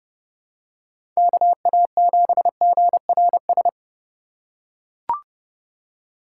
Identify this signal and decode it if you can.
Morse code, heard as a keyed tone, about 30 wpm, 700 Hz. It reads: KA7GRS
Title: USA Call Signs - 4x Character Spacing - 30wpm